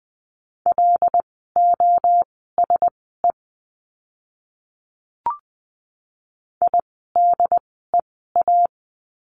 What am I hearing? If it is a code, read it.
Morse code, heard as a keyed tone, about 20 wpm, spaced as for 14 wpm, 700 Hz. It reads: LOSE IDEA
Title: 108 Lesson 18f - Letter m - Words 20wpm